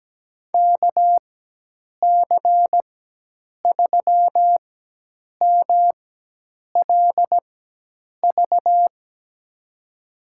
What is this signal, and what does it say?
Morse code, heard as a keyed tone, about 17 wpm, spaced as for 8 wpm, 700 Hz. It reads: KC3MLV
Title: USA Call Signs - 4x Character Spacing - 17wpm